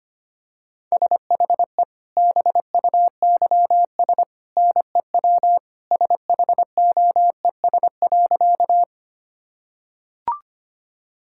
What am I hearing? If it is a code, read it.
Morse code, heard as a keyed tone, about 25 wpm, 690 Hz. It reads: SHE BUYS NEW SHOES.
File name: Easy Sentences with Hints - 25wpm